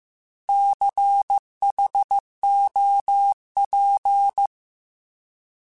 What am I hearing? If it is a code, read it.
Morse code, heard as a keyed tone, about 15 wpm, 800 Hz. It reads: CHOP